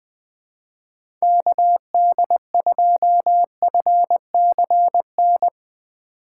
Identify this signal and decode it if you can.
Morse code, heard as a keyed tone, about 20 wpm, 700 Hz. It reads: KD2FCN
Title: USA Call Signs - 6 Characters - 4x Character Spacing - 20wpm